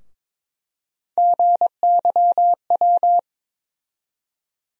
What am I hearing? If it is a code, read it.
Morse code, heard as a keyed tone, about 21 wpm, 700 Hz. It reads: GYW
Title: USA Call Sign Postfixes - 3 Letters 22wpm